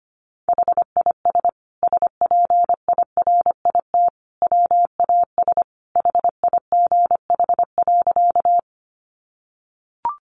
Morse code, heard as a keyed tone, about 25 wpm, 700 Hz. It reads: HIS SPIRIT WAS HIGH.
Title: Sentences from Top 1400 Words 25wpm